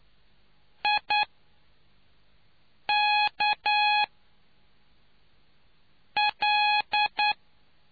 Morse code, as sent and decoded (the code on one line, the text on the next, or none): .. -.- .-..
IKL